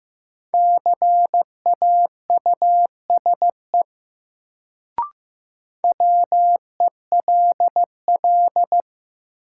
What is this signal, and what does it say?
Morse code, heard as a keyed tone, about 15 wpm, 700 Hz. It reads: CAUSE WELL
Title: Sentences from Top 200 Words - New Word Drill - 15wpm